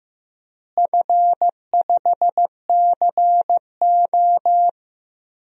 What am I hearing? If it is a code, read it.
Morse code, heard as a keyed tone, about 15 wpm, 700 Hz. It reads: F5CO
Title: DXCC-100 Call Signs - 4 Characters - Repeated 2x - 15wpm